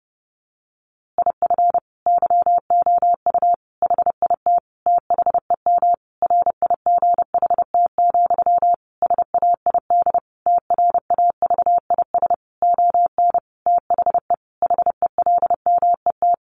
Morse code, heard as a keyed tone, about 30 wpm, 700 Hz. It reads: IF YOU HIT THEM RIGHT, SAID TRAVIS ON THE HELMET
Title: A Sound Of Thunder 30wpm